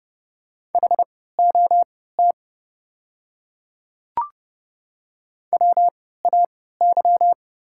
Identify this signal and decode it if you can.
Morse code, heard as a keyed tone, about 30 wpm, spaced as for 17 wpm, 700 Hz. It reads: HOT WAY
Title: 3x Character Spacing - Top 100 Words - 30wpm